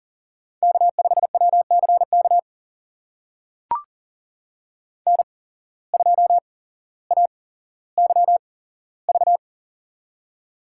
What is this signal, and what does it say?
Morse code, heard as a keyed tone, about 37 wpm, spaced as for 17 wpm, 700 Hz. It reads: K5WCK N2AYV
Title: USA Call Signs - 8x Character Spacing - 40wpm